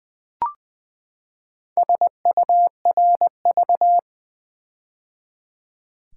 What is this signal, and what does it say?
Morse code, heard as a keyed tone, about 20 wpm, 700 Hz. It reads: SURV
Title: Build A Word - 4 Letters At A Time - 12-Letter Words - 20wpm